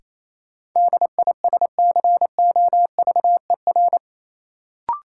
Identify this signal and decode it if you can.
Morse code, heard as a keyed tone, about 28 wpm, 700 Hz. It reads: DISCOVER
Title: Sentences from Top 1100 Words - New Word Drill - 28wpm